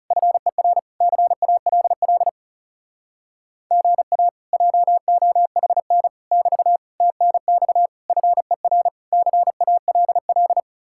Morse code, heard as a keyed tone, about 40 wpm, 700 Hz. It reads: FER CALL GA JOHN <BT> TNX FER CALL